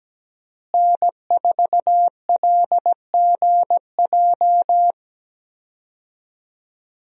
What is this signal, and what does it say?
Morse code, heard as a keyed tone, about 17 wpm, 700 Hz. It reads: N4LGJ
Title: USA Call Signs - 5 Characters - Repeated 3x - 17wpm